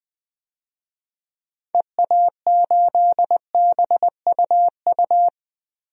Morse code, heard as a keyed tone, about 20 wpm, 700 Hz. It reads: EA8BUU
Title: DXCC-100 Call Signs - 6 Characters - 20wpm